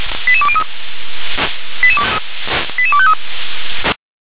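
Five-tone selective call (ZVEI), heard as quick runs of five tones, tones of about 70 ms. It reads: 88203 88253
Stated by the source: Five-tone ZVEI selective call received by SDR.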